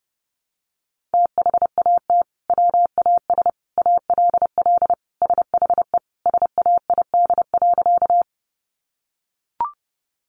Morse code, heard as a keyed tone, about 30 wpm, 700 Hz. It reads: THAT WAS ALL SHE SAID.
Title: Sentences from Top 100 Words 30wpm